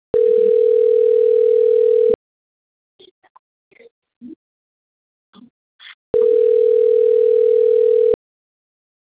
Ringback tone (North America), 440 + 480 Hz.